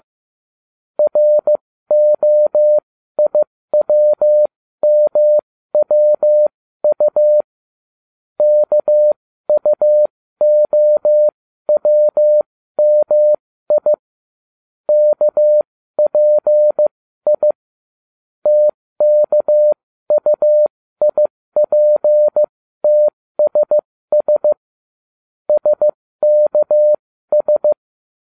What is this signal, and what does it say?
Morse code, heard as a keyed tone, about 15 wpm, spaced as for 13 wpm, 600 Hz. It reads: ROIWMWU KUOWMI KPI TKUIPTSS SKS